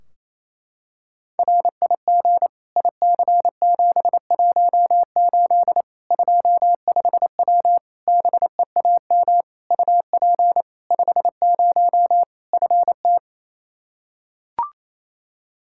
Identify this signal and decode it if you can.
Morse code, heard as a keyed tone, about 28 wpm, 700 Hz. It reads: RIG IC718 25W BEAM UP 50 FT
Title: K1USN SST QSO Practice 28wpm